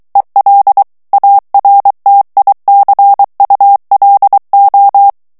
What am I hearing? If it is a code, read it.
Morse code, heard as a keyed tone, about 23 wpm, 800 Hz. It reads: EL ARTICULO
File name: morse mp3 / ESPAGNOL TEXTE EN CLAIR 2 VITESSE 22 MOTS MINUTES